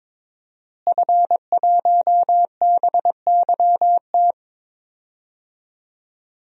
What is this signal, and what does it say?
Morse code, heard as a keyed tone, about 22 wpm, 700 Hz. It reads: F1BYT
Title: DXCC-100 Call Signs - 5 Characters - Repeated 3x - 22wpm